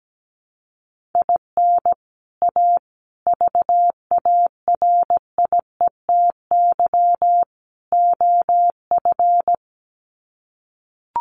Morse code, heard as a keyed tone, about 17 wpm, 700 Hz. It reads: IN A VARIETY OF
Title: Sets of 4 Words 17wpm